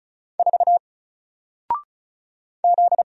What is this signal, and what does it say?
Morse code, heard as a keyed tone, about 36 wpm, 700 Hz. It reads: 4 Z
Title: Single Letter-Number Alt 35wpm